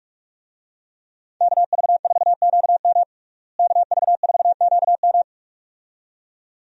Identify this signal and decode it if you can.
Morse code, heard as a keyed tone, about 46 wpm, 700 Hz. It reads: KV4QM KV4QM
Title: USA Call Signs - 5 Characters - Repeated 2x - 45wpm